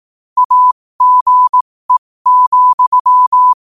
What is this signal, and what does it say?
Morse code, heard as a keyed tone, about 18 wpm, spaced as for 15 wpm, 1000 Hz. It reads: AGE,